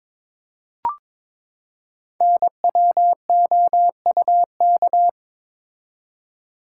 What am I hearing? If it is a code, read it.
Morse code, heard as a keyed tone, about 23 wpm, 690 Hz. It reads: NWOUK
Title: Code Groups - 5 Letters - 22wpm